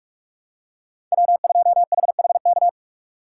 Morse code, heard as a keyed tone, about 43 wpm, 700 Hz. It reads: W2HHK
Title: USA Call Signs - 5 Characters - 2x Character Spacing - 45wpm